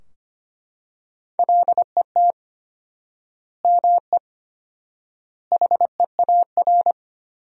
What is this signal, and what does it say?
Morse code, heard as a keyed tone, about 25 wpm, 700 Hz. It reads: LET ME HEAR
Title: 4x Word-Spacing - Sentences from Top 300 Words - 25wpm